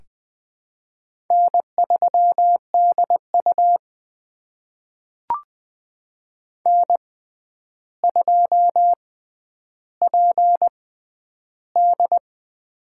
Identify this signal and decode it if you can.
Morse code, heard as a keyed tone, about 20 wpm, spaced as for 10 wpm, 700 Hz. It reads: N3DU N2PD
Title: USA Call Signs - 6x Character Spacing - 20wpm